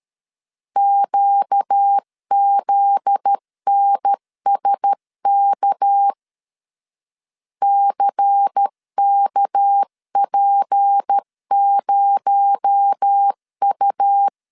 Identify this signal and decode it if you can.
Morse code, heard as a keyed tone, about 13 wpm, 800 Hz. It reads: QZNSK CKP0U
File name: LEZIONI / 7 GF9Z / 7 GF9Z 12wpm 2147